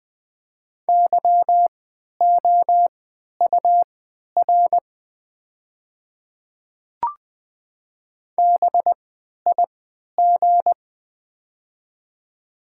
Morse code, heard as a keyed tone, about 20 wpm, spaced as for 11 wpm, 700 Hz. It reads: YOUR BIG
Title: 3x Character Spacing - Top 200 Words - 20wpm